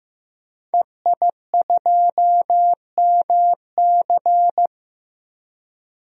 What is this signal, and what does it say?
Morse code, heard as a keyed tone, about 16 wpm, 700 Hz. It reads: EI2MC
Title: DXCC-100 Call Signs - 5 Characters - Repeated 2x - 15wpm